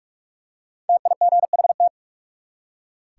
Morse code, heard as a keyed tone, about 43 wpm, 700 Hz. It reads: TIGHT
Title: Five Letter Words - 45wpm